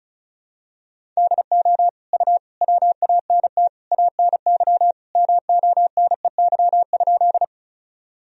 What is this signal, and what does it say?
Morse code, heard as a keyed tone, about 36 wpm, 700 Hz. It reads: DO U WANT ANY MONEY?
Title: Sentences from Top 400 Words and Abbreviations 35wpm